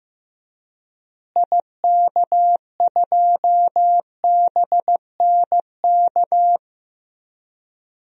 Morse code, heard as a keyed tone, about 15 wpm, 700 Hz. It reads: IK2BNK
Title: DXCC-100 Call Signs - 6 Characters - Repeated 2x - 15wpm